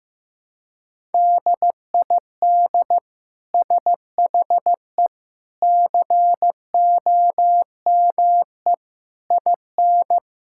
Morse code, heard as a keyed tone, about 15 wpm, 700 Hz. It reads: DID SHE COME IN
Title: Sentences from Top 1000 Words - Review Part 5 - 15wpm